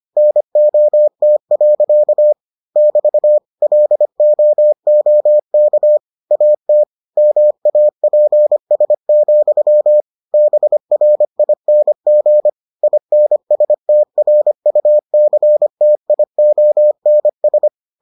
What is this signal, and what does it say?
Morse code, heard as a keyed tone, about 25 wpm, 600 Hz. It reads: NOT. <BT> LOOK AT MAPS, BRING INSTRUCTIONS